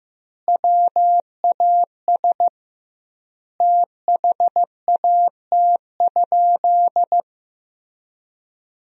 Morse code, heard as a keyed tone, about 16 wpm, 700 Hz. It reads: WAS THAT?
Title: 2x Word-Spacing - Sentences from Top 100 Words - 15wpm